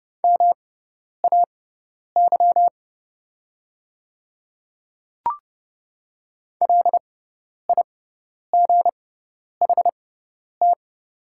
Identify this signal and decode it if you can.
Morse code, heard as a keyed tone, about 30 wpm, spaced as for 10 wpm, 700 Hz. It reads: MAY LIGHT
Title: 6x Character Spacing - Top 500 Words - 30wpm